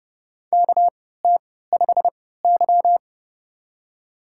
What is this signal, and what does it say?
Morse code, heard as a keyed tone, about 30 wpm, spaced as for 17 wpm, 700 Hz. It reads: KT5Y
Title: USA Call Signs - 4 Characters - 3x Character Spacing - 30wpm